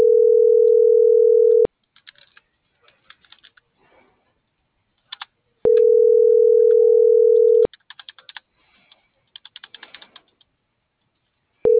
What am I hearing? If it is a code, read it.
Ringback tone (North America), 440 + 480 Hz.